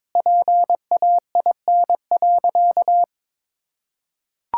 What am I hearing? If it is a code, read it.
Morse code, heard as a keyed tone, about 22 wpm, 700 Hz. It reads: PAIN.
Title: Sentences from Top 1600 Words 22wpm